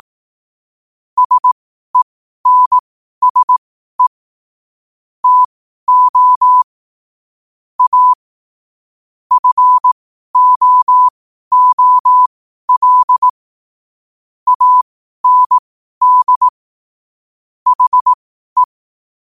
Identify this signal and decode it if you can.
Morse code, heard as a keyed tone, about 18 wpm, spaced as for 12 wpm, 1000 Hz. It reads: SENSE TO A FOOL AND HE
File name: qotdfiles / 12wpm